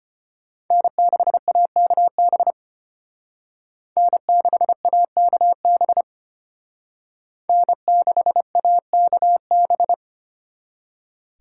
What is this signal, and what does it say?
Morse code, heard as a keyed tone, about 29 wpm, 700 Hz. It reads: N6AKB N6AKB N6AKB
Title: Speed Racing - USA 1x3 Call Signs 25wpm